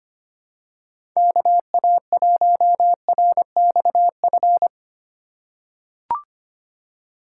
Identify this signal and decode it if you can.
Morse code, heard as a keyed tone, about 25 wpm, 700 Hz. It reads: KA1RXF
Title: USA Call Signs - 6 Characters - 2x Character Spacing - 25wpm